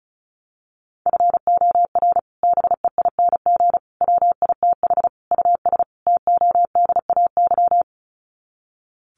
Morse code, heard as a keyed tone, about 34 wpm, 700 Hz. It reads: FOR BEING WITH US TODAY
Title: Sets of 5 Words Encore2 35wpm